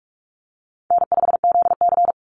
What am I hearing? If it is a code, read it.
Morse code, heard as a keyed tone, about 44 wpm, 700 Hz. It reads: N5ZC
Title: NA SOTA Chaser Callsigns 45wpm